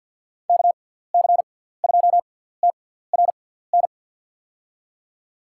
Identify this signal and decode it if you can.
Morse code, heard as a keyed tone, about 52 wpm, spaced as for 17 wpm, 700 Hz. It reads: KC2TRN
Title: USA Call Signs - 6 Characters - 6x Character Spacing - 50wpm